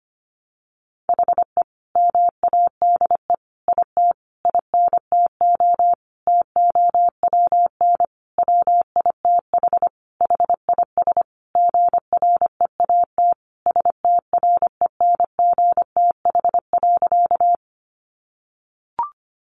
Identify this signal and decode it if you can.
Morse code, heard as a keyed tone, about 25 wpm, 700 Hz. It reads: HE MADE IT INTO TOWN WITH HIS GREAT STRENGTH.
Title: Sentences from Top 1300 Words 25wpm